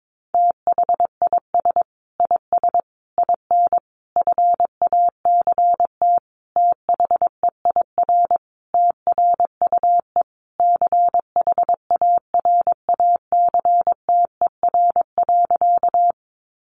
Morse code, heard as a keyed tone, about 22 wpm, 700 Hz. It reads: THIS IS IN FACT THEIR TRUE CHARACTER.
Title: Sentences from Top 1200 Words 22wpm